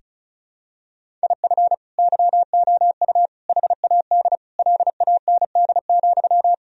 Morse code, heard as a keyed tone, about 36 wpm, 700 Hz. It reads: IF YOU HAD LAND,
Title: Sentences from Top 1000 Words - Review Part 1 - 35wpm